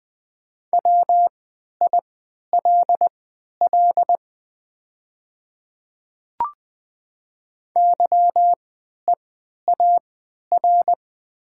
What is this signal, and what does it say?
Morse code, heard as a keyed tone, about 20 wpm, spaced as for 11 wpm, 700 Hz. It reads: WILL YEAR
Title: 3x Character Spacing - Top 300 Words - 20wpm